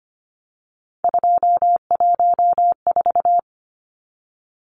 Morse code, heard as a keyed tone, about 25 wpm, 700 Hz. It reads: 214
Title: Numbers - 3-Digit 25wpm